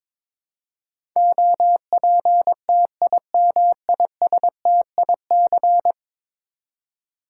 Morse code, heard as a keyed tone, about 22 wpm, 700 Hz. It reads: OPTIMISTIC
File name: practice / 22wpm / 10 Letter Words - First 3 Letters Given - 22wpm